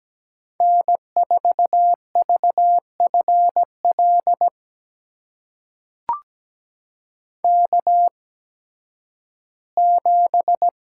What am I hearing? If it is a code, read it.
Morse code, heard as a keyed tone, about 17 wpm, spaced as for 11 wpm, 700 Hz. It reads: N4VFL K7
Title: USA Call Signs - 5 Characters - 8x Character Spacing - 17wpm